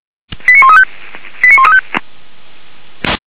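Five-tone selective call (ZVEI), heard as quick runs of five tones, tones of about 70 ms. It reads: 89135 89135
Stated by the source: ZVEI five-tone selective call received on an SDR.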